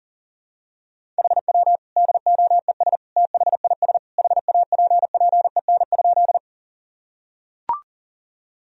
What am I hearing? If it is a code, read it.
Morse code, heard as a keyed tone, about 37 wpm, 700 Hz. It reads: HW DOES THIS HAPPEN?
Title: Sentences from Top 400 Words and Abbreviations 40wpm